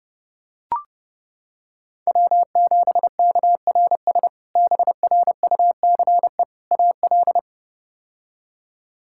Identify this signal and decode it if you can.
Morse code, heard as a keyed tone, about 32 wpm, 700 Hz. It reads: W7KRS BRUCE AL
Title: K1USN SST QSO Practice - 30wpm